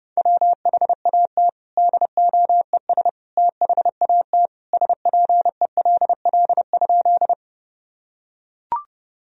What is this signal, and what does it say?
Morse code, heard as a keyed tone, about 30 wpm, 700 Hz. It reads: WHAT DOES THAT SPELL?
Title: Sentences from Top 1000 Words - Review Part 1 - 30wpm